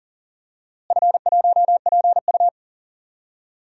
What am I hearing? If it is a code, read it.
Morse code, heard as a keyed tone, about 40 wpm, 700 Hz. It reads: F1PU
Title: DXCC-100 Call Signs - 4 Characters - Repeated 3x - 40wpm